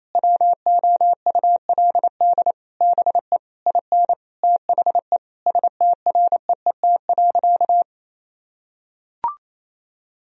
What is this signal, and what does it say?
Morse code, heard as a keyed tone, about 27 wpm, 690 Hz. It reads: WOULD BE IN THE STREET.
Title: Sentences from Top 1000 Words - Review Part 3 - 28wpm